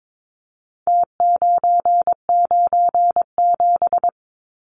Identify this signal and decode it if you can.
Morse code, heard as a keyed tone, about 22 wpm, 700 Hz. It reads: T997
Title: Numbers - Contest Style - 4-Digit 22wpm